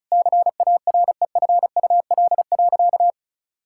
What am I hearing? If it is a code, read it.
Morse code, heard as a keyed tone, about 36 wpm, 700 Hz. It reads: CAREFUL.
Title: Sentences from Top 1600 Words 35wpm